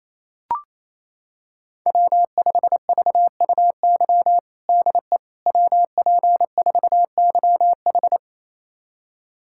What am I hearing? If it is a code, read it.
Morse code, heard as a keyed tone, about 27 wpm, 700 Hz. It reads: W5VUY DE WP4YH
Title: USA Callsign DE Callsign - 5 Characters - 28wpm